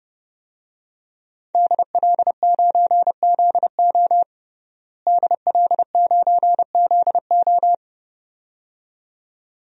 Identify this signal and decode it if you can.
Morse code, heard as a keyed tone, about 30 wpm, 700 Hz. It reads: DL9ZO DL9ZO
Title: DXCC-100 Call Signs - 5 Characters - Repeated 2x - 30wpm